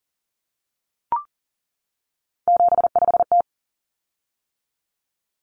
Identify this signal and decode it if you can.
Morse code, heard as a keyed tone, about 40 wpm, 700 Hz. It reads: 75T
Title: Common Cut Numbers - 3-Digit 40wpm